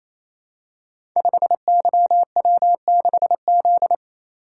Morse code, heard as a keyed tone, about 29 wpm, 700 Hz. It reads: 5YW6Z
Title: Code Groups - 5 Letter:Numbers - 28wpm